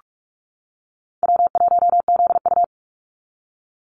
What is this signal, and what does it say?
Morse code, heard as a keyed tone, about 44 wpm, 700 Hz. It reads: W1ZU
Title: NA SOTA Activator Callsigns 45wpm